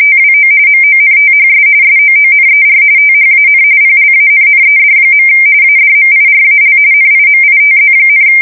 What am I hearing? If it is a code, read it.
RTTY, about 45.45 baud. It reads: TWO MONTHS SOONER THAN PLANNED. ADMIRAL NIMITZ,
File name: fsk170 / USS-New-Jersey